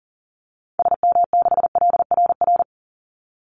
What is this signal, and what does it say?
Morse code, heard as a keyed tone, about 40 wpm, 700 Hz. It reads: SM6LRR
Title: Call Signs Repeated 2x - 40wpm